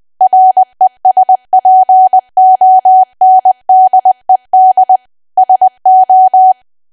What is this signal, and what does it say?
Morse code, heard as a keyed tone, about 20 wpm, 750 Hz. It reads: RESPONDED SO
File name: Archive / 20 WPM / 20WPM